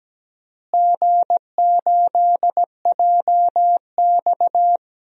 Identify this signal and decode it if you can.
Morse code, heard as a keyed tone, about 17 wpm, 700 Hz. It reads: G8JX
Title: DXCC-100 Call Signs - 4 Characters - Repeated 2x - 17wpm